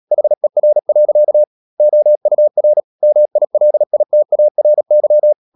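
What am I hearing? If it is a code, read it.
Morse code, heard as a keyed tone, about 37 wpm, 600 Hz. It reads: HER. OUR MILITARY
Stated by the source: Morse code practice at 35–39 wpm sending English text of Ken Follett's novel Never, from CW med Gnister.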